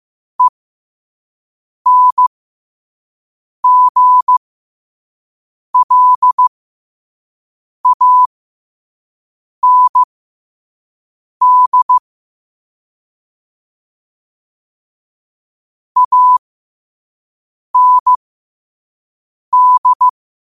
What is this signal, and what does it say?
Morse code, heard as a keyed tone, about 15 wpm, spaced as for 5 wpm, 1000 Hz. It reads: ENGLAND AND